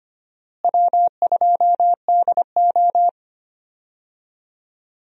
Morse code, heard as a keyed tone, about 25 wpm, 700 Hz. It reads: W2DO
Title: USA Call Signs - 4 Characters - Repeated 3x - 25wpm